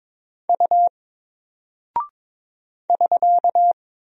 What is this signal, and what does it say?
Morse code, heard as a keyed tone, about 22 wpm, 700 Hz. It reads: U <SK>
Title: Single Character (Letter:Number:Punctuation:Prosign) Alt 22wpm